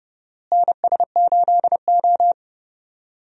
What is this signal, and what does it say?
Morse code, heard as a keyed tone, about 32 wpm, 700 Hz. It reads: NS8O
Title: USA Call Signs - 4 Characters - Repeated 2x - 30wpm